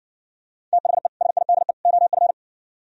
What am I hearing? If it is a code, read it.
Morse code, heard as a keyed tone, about 59 wpm, 700 Hz. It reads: THE SIDE OF